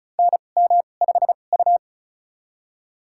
Morse code, heard as a keyed tone, about 36 wpm, spaced as for 25 wpm, 700 Hz. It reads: NM5U
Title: USA Call Signs - 4 Characters - 2x Character Spacing - 35wpm